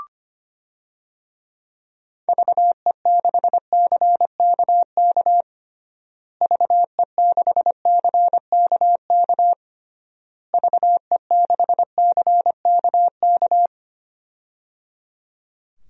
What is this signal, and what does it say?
Morse code, heard as a keyed tone, about 25 wpm, 700 Hz. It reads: VE6CKK VE6CKK VE6CKK
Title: DXCC-100 Call Signs - 6 Characters - Repeated 3x - 25wpm